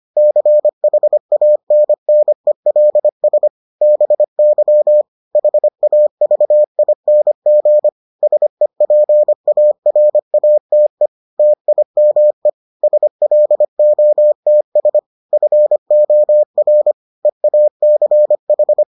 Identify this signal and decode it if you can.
Morse code, heard as a keyed tone, about 25 wpm, 600 Hz. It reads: CHANNELS BY HAVING SEPARATE TIME SLOTS FOR EACH